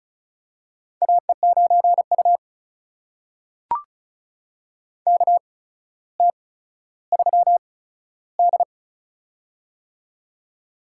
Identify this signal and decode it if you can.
Morse code, heard as a keyed tone, about 36 wpm, spaced as for 15 wpm, 700 Hz. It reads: AE9U KT3D
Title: USA Call Signs - 4 Characters - 8x Character Spacing - 35wpm